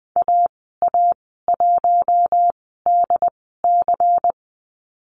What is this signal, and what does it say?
Morse code, heard as a keyed tone, about 19 wpm, spaced as for 14 wpm, 700 Hz. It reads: AA1DC
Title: 140 Lesson 23g - Letter f - Call Signs 20wpm